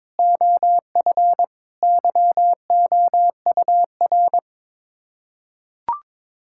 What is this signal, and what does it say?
Morse code, heard as a keyed tone, about 21 wpm, 700 Hz. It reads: OF YOUR
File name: practice / 22wpm / Sets of 7 Words - 22wpm